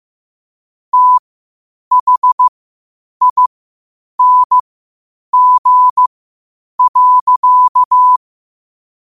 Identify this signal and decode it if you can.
Morse code, heard as a keyed tone, about 15 wpm, spaced as for 8 wpm, 1000 Hz. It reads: THING.
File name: qotdfiles / 8wpm